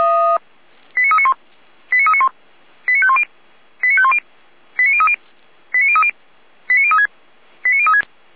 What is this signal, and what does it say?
Five-tone selective call (ZVEI), heard as quick runs of five tones, tones of about 70 ms. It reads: 89391 89391 89510 89510 89030 89030 89036 89036